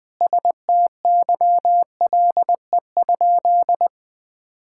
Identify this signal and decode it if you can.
Morse code, heard as a keyed tone, about 20 wpm, 700 Hz. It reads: STYLE?
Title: Sentences from Top 1000 Words - Review Part 5 - 20wpm